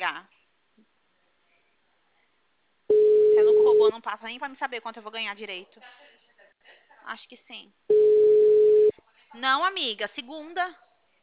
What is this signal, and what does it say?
Ringback tone (Europe, 425 Hz), with speech.
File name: out- -2019- - - .21995